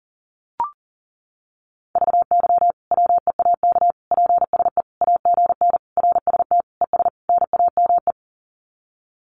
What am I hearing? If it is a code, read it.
Morse code, heard as a keyed tone, about 40 wpm, 700 Hz. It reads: VY WEAK PSE AGN RST ES NAME
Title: Common QSO Phrases 40wpm